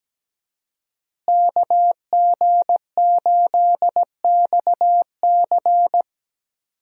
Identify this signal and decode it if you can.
Morse code, heard as a keyed tone, about 17 wpm, 700 Hz. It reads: KG8XC
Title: USA Call Signs - 5 Characters - Repeated 3x - 17wpm